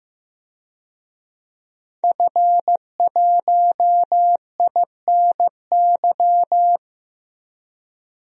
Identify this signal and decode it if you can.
Morse code, heard as a keyed tone, about 15 wpm, 700 Hz. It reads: F1INY
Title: DXCC-100 Call Signs - 5 Characters - 15wpm